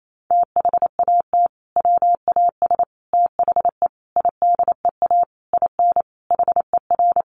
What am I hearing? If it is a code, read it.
Morse code, heard as a keyed tone, about 28 wpm, 700 Hz. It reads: THAT WAS THE IDEA IN HER
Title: Sentences from Top 1100 Words 28wpm